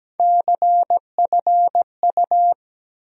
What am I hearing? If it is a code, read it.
Morse code, heard as a keyed tone, about 17 wpm, 700 Hz. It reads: CFU
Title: USA Call Sign Postfixes - 3 Letters 17wpm